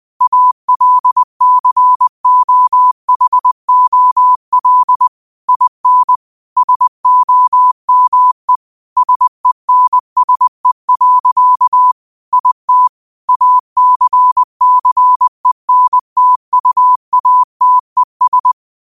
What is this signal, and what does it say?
Morse code, heard as a keyed tone, about 20 wpm, 1000 Hz. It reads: ALCOHOL IN SOME SENSE. IT ACCENTUATES